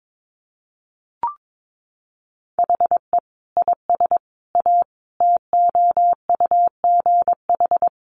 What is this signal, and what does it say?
Morse code, heard as a keyed tone, about 22 wpm, 700 Hz. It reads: HE IS A TOUGH
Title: Sentences from Top 1200 Words 22wpm